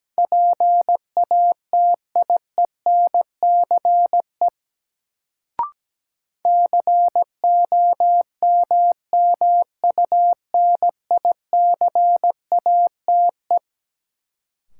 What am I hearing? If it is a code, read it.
Morse code, heard as a keyed tone, about 17 wpm, 700 Hz. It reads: PATIENCE COMMUNICATE
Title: Sentences from Top 1300 Words - New Word Drill - 17wpm